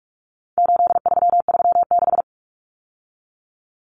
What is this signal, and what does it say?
Morse code, heard as a keyed tone, about 44 wpm, 700 Hz. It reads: 8336
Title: Numbers - 4-Digit 45wpm